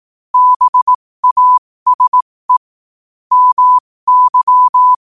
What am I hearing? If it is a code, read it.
Morse code, heard as a keyed tone, about 18 wpm, spaced as for 15 wpm, 1000 Hz. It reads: BASE MY
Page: New quotes every day in morse code at 15 Words per minute.